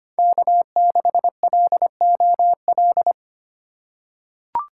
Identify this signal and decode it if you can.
Morse code, heard as a keyed tone, about 25 wpm, 700 Hz. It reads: K6LOL
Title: USA Call Signs - 5 Characters - 2x Character Spacing - 25wpm